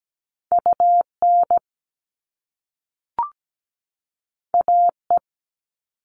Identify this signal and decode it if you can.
Morse code, heard as a keyed tone, about 17 wpm, 700 Hz. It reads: UN AE
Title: All Cut Numbers - 2-Digit 17wpm